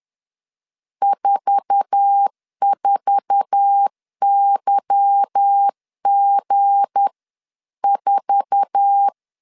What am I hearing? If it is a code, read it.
Morse code, heard as a keyed tone, about 10 wpm, 800 Hz. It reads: 44YG4
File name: LEZIONI / 8 YJ45 / 8 YJ4 10wpm 2633